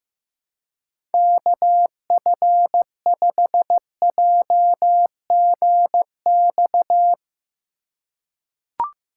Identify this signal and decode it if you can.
Morse code, heard as a keyed tone, about 16 wpm, 700 Hz. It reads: KF5JGX
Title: USA Call Signs - 2x Character Spacing - 15wpm